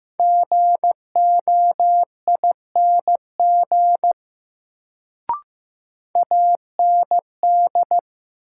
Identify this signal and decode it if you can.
Morse code, heard as a keyed tone, about 15 wpm, 700 Hz. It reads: GOING AND
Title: Sets of 2 Words - New Word Drill - 15wpm